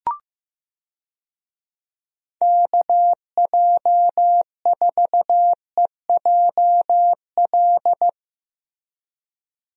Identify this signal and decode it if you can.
Morse code, heard as a keyed tone, about 15 wpm, 700 Hz. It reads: KJ4EJL
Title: USA Call Signs - 6 Characters - Repeated 2x - 15wpm